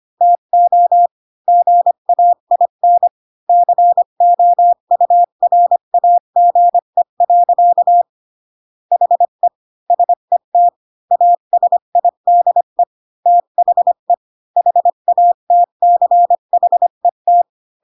Morse code, heard as a keyed tone, about 25 wpm, 700 Hz. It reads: TO GAIN COURAGE. HE SET ASIDE THE HATCHET
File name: Pinocchio / Pinocchio ch1 20 25 ik1hge